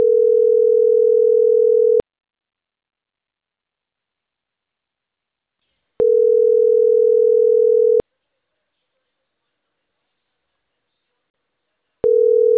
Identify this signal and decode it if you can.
Ringback tone (North America), 440 + 480 Hz.